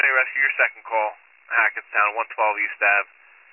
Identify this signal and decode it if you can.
Emergency Alert System (SAME) tones, heard as harsh buzzing bursts, with speech. It is heard from 0.3 to 3.0 s.